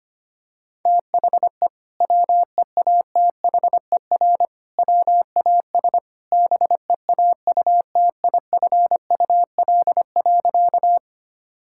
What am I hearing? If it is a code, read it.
Morse code, heard as a keyed tone, about 25 wpm, 700 Hz. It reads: THE WEATHER WAS BEAUTIFUL.
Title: Sentences from Top 1000 Words - Review Part 5 - 25wpm